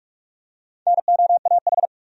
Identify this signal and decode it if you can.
Morse code, heard as a keyed tone, about 46 wpm, 700 Hz. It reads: NOAH
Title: USA Common Names - 45wpm